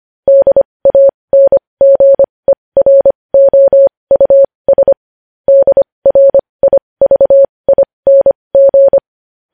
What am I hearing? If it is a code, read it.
Morse code, heard as a keyed tone, about 25 wpm, spaced as for 20 wpm, 550 Hz. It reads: DANGEROUS DRIVING
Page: news-headlines-in-morse-code-at-20-wpm